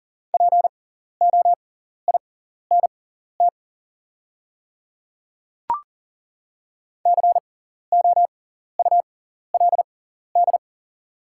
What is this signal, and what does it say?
Morse code, heard as a keyed tone, about 40 wpm, spaced as for 14 wpm, 700 Hz. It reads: POINT COULD
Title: 6x Character Spacing - Top 200 Words - 40wpm